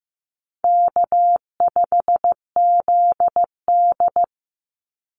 Morse code, heard as a keyed tone, about 15 wpm, 700 Hz. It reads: K5ZD
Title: Easy Call Signs - 15wpm